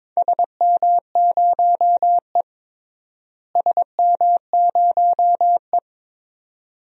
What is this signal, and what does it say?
Morse code, heard as a keyed tone, about 22 wpm, 700 Hz. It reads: SM0E SM0E
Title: DXCC-100 Call Signs - 4 Characters - Repeated 2x - 22wpm